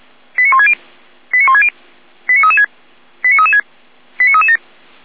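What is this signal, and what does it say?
Five-tone selective call (ZVEI), heard as quick runs of five tones, tones of about 70 ms. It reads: 89160 89160 89336 89336 89337
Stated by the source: Five-tone ZVEI selective call received on an SDR.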